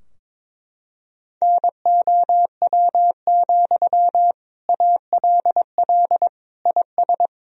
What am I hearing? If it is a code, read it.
Morse code, heard as a keyed tone, about 22 wpm, 700 Hz. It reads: NOW, ALL IS
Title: Sentences from Top 1000 Words - Review Part 4 - 22wpm